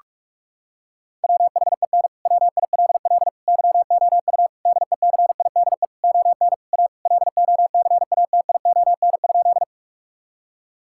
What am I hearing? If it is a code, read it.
Morse code, heard as a keyed tone, about 45 wpm, 700 Hz. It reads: WHEN WILL YOU DECIDE ON A LOCATION?
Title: Sentences from Top 1000 Words - Review Part 6 - 45wpm